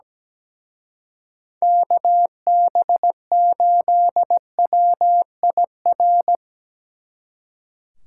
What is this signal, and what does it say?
Morse code, heard as a keyed tone, about 19 wpm, 700 Hz. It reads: KB8WIR
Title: Speed Racing - USA 2x3 Call Signs 17wpm